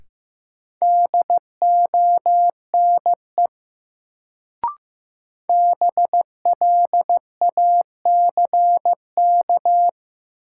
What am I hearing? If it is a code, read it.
Morse code, heard as a keyed tone, about 15 wpm, 700 Hz. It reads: DONE BLACK
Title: Sets of 5 Words Encore - New Word Drill - 15wpm